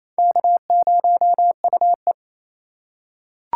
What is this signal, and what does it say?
Morse code, heard as a keyed tone, about 28 wpm, 700 Hz. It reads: K0UE
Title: USA Call Signs - 4 Characters - 2x Character Spacing - 28wpm